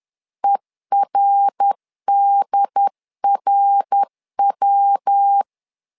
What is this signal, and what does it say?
Morse code, heard as a keyed tone, about 10 wpm, 800 Hz. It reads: ERDRW
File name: LEZIONI / 3 MWRO / 3 MWR 10wpm 1952